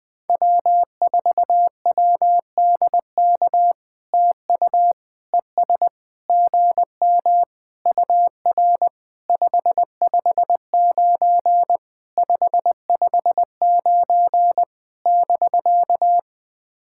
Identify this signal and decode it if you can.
Morse code, heard as a keyed tone, about 20 wpm, 700 Hz. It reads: W4WDK TU ES GM UR 559 559 <BK>
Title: POTA QSO Practice - 20wpm